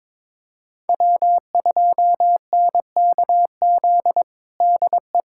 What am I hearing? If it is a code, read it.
Morse code, heard as a keyed tone, about 22 wpm, 700 Hz. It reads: W2NKZ DE
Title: USA Callsign DE Callsign 22wpm